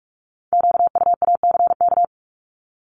Morse code, heard as a keyed tone, about 46 wpm, 700 Hz. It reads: QUACK
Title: Multi-letter Phonemes 45wpm